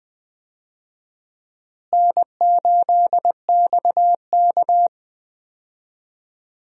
Morse code, heard as a keyed tone, about 20 wpm, 700 Hz. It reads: N8XK
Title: USA Call Signs - 4 Characters - Repeated 2x - 20wpm